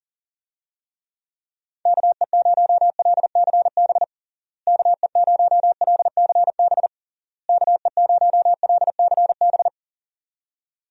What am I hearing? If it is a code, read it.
Morse code, heard as a keyed tone, about 37 wpm, 700 Hz. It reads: KE0LCB KE0LCB KE0LCB
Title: USA Call Signs - 6 Characters - Repeated 3x - 40wpm